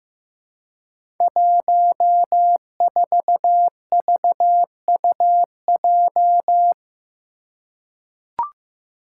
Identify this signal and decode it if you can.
Morse code, heard as a keyed tone, about 15 wpm, 700 Hz. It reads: 14VUJ
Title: Five Challenging Letter-Number Combinations 15wpm